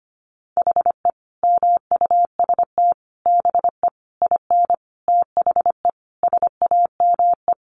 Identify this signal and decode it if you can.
Morse code, heard as a keyed tone, about 25 wpm, 700 Hz. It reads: HE MUST BE IN THE SAME
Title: Sentences from Top 500 Words 25wpm